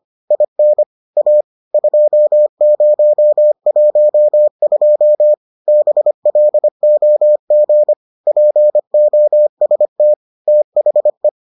Morse code, heard as a keyed tone, about 25 wpm, 600 Hz. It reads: IN A 2012 BLOG POST THE